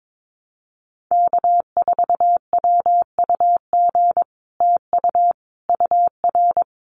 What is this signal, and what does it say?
Morse code, heard as a keyed tone, about 22 wpm, 700 Hz. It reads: K4WUG TU UR
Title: Common QSO Phrases Encore 22wpm